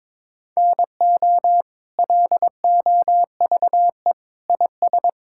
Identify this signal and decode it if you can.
Morse code, heard as a keyed tone, about 22 wpm, 700 Hz. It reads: NO LOVE IS
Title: Sentences from Top 1600 Words 22wpm